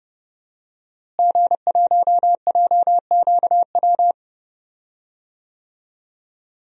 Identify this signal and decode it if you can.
Morse code, heard as a keyed tone, about 30 wpm, 700 Hz. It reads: G1JQW
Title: DXCC-100 Call Signs - 5 Characters - Repeated 3x - 30wpm